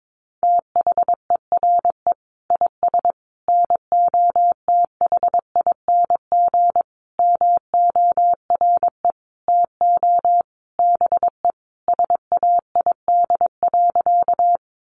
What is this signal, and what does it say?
Morse code, heard as a keyed tone, about 22 wpm, 700 Hz. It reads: THERE IS NOTHING MORE TO BE SAID.
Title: Sentences from Top 500 Words 22wpm